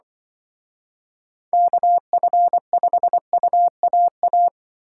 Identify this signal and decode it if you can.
Morse code, heard as a keyed tone, about 24 wpm, 700 Hz. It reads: KF5UAA
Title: Speed Racing - USA 2x3 Call Signs 20wpm